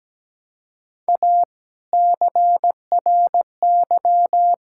Morse code, heard as a keyed tone, about 17 wpm, 700 Hz. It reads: A CRY
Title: Sentences from Top 1000 Words - Review Part 5 - 17wpm